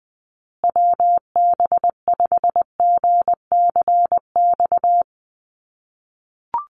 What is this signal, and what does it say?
Morse code, heard as a keyed tone, about 20 wpm, 700 Hz. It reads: WB5GCX
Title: USA Call Signs Repeated 2x 20wpm